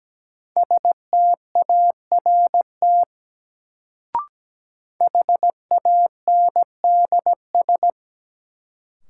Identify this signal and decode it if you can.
Morse code, heard as a keyed tone, about 17 wpm, 700 Hz. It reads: START HANDS
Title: Sets of 4 Words - New Word Drill - 17wpm